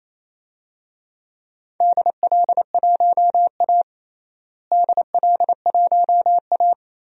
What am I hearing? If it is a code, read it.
Morse code, heard as a keyed tone, about 28 wpm, 700 Hz. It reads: DL1A DL1A
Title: DXCC-100 Call Signs - 4 Characters - Repeated 3x - 28wpm